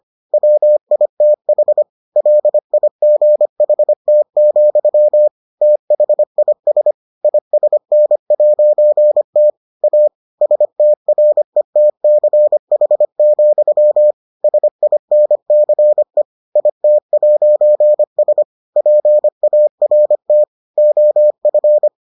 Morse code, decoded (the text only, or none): WITH LIGHT, THIS ISN'T A STRETCH, SINCE IT'S PART OF